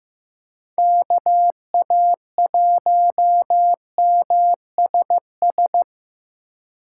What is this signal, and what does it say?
Morse code, heard as a keyed tone, about 15 wpm, 700 Hz. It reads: KA1MSS
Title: USA Call Signs - 6 Characters - 2x Character Spacing - 15wpm